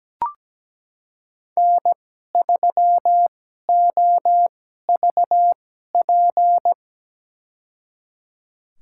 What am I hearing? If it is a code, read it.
Morse code, heard as a keyed tone, about 17 wpm, spaced as for 12 wpm, 700 Hz. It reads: N3OVP
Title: USA Call Signs - 5 Characters - 2x Character Spacing - 17wpm